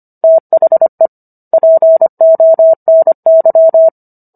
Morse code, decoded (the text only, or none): THE PONY